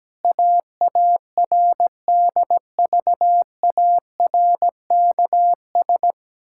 Morse code, heard as a keyed tone, about 17 wpm, 700 Hz. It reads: AARDVARKS
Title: Pangram Challenge v2 - 17wpm